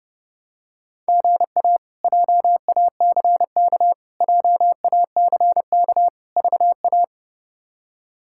Morse code, heard as a keyed tone, about 32 wpm, 700 Hz. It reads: GA JACK JACK VA
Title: K1USN SST QSO Practice - 30wpm